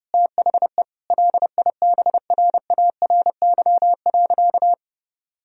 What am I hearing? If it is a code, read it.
Morse code, heard as a keyed tone, about 30 wpm, 700 Hz. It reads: THE LIBRARY.
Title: Sentences from Top 1000 Words - Review Part 1 - 30wpm